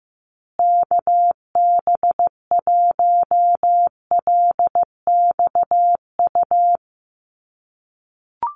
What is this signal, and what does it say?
Morse code, heard as a keyed tone, about 15 wpm, 700 Hz. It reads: KB1LXU
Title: USA Call Signs Repeated 3x 15wpm